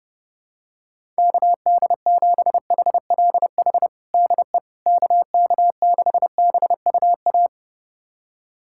Morse code, heard as a keyed tone, about 32 wpm, 700 Hz. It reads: KD7HLH DE KK6BUA
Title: USA Callsign DE Callsign - 6 Characters - 30wpm